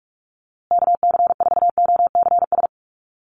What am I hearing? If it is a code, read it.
Morse code, heard as a keyed tone, about 44 wpm, 700 Hz. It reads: KC4OCS
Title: USA Call Signs Repeated 2x 45wpm